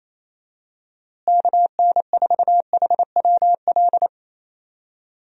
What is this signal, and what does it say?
Morse code, heard as a keyed tone, about 28 wpm, 700 Hz. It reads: KN4HWL
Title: USA Call Signs - 6 Characters - Repeated 3x - 28wpm